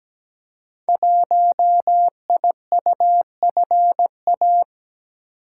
Morse code, heard as a keyed tone, about 17 wpm, 700 Hz. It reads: 1IUFA
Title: Code Groups - 5 Letter:Numbers - 17wpm